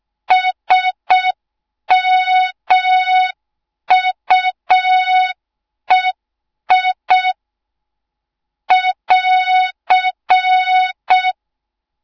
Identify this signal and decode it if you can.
Morse code, heard as a keyed tone, about 6 wpm, 3700 Hz. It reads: SMUEI <AR>
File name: Morze-HA2MN / hang / 06BDU HA2MN